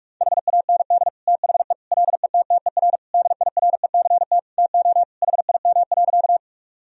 Morse code, heard as a keyed tone, about 46 wpm, 700 Hz. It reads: HAND THE LETTER DIRECT TO HIM.
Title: Sentences from Top 1000 Words - Review Part 2 - 45wpm